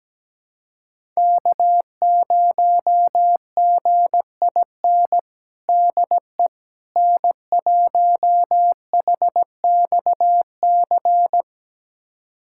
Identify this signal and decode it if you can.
Morse code, heard as a keyed tone, about 17 wpm, 700 Hz. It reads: K0GIN DE N1HXC
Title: USA Callsign DE Callsign - 5 Characters - 17wpm